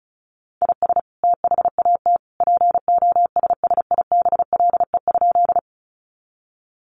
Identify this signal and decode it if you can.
Morse code, heard as a keyed tone, about 34 wpm, 700 Hz. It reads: IS THAT POSSIBLE?
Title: Sentences from Top 500 Words 35wpm